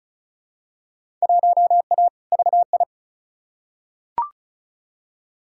Morse code, heard as a keyed tone, about 36 wpm, 700 Hz. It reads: 1A VI
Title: Field Day Exchanges Encore - 35wpm